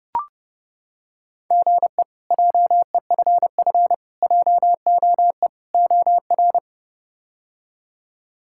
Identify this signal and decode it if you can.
Morse code, heard as a keyed tone, about 30 wpm, 700 Hz. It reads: GE JEFF JOE OR
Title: K1USN SST QSO Practice - 30wpm